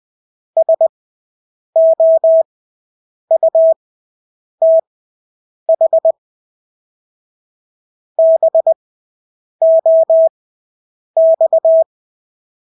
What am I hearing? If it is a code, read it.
Morse code, heard as a keyed tone, about 20 wpm, spaced as for 8 wpm, 650 Hz. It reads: SOUTH BOX
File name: Engelske ord 20/8 wpm